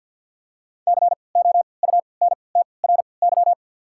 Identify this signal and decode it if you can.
Morse code, heard as a keyed tone, about 50 wpm, spaced as for 28 wpm, 700 Hz. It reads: COUNTRY
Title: 3x Character Spacing - Top 500 Words - 50wpm